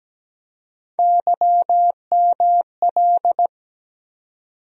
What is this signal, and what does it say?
Morse code, heard as a keyed tone, about 17 wpm, 700 Hz. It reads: YML
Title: USA Call Sign Postfixes - 3 Letters 17wpm